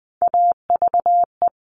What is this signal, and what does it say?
Morse code, heard as a keyed tone, about 20 wpm, 700 Hz. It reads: AVE
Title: Top Three Letter Combinations in Words 20wpm